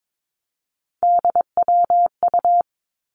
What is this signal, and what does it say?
Morse code, heard as a keyed tone, about 22 wpm, 700 Hz. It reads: DWU
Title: Three Challenging Letter-Number Combinations 22wpm